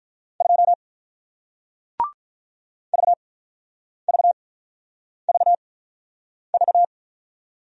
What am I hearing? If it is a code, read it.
Morse code, heard as a keyed tone, about 48 wpm, spaced as for 9 wpm, 700 Hz. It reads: 2 VVVV